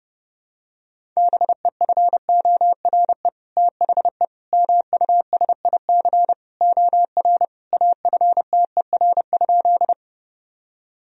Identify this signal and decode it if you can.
Morse code, heard as a keyed tone, about 30 wpm, 700 Hz. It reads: BEFORE THE MUSIC OR AFTER?
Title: Sentences from Top 1000 Words - Review Part 1 - 30wpm